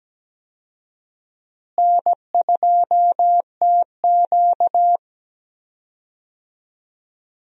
Morse code, heard as a keyed tone, about 17 wpm, 700 Hz. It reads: N2TQ
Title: USA Call Signs - 4 Characters - Repeated 2x - 17wpm